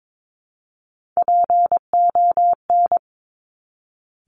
Morse code, heard as a keyed tone, about 22 wpm, 700 Hz. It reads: PON
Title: Top Three Letter Combinations in Words 22wpm